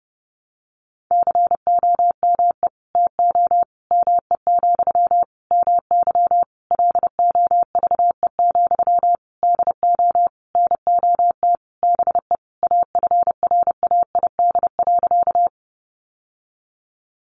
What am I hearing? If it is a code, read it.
Morse code, heard as a keyed tone, about 30 wpm, 700 Hz. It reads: COME TO ME, MY LOVE, DO NOT BE AFRAID.
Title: Sentences from Top 1500 Words 30wpm